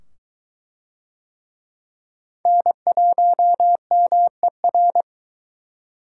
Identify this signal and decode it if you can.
Morse code, heard as a keyed tone, about 19 wpm, 700 Hz. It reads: N1MER